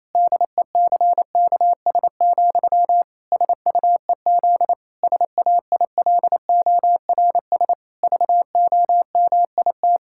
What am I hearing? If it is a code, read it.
Morse code, heard as a keyed tone, about 28 wpm, 700 Hz. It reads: DECKS, SUEZ SAILORS VOMIT
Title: Pangram Challenge v2 - 28wpm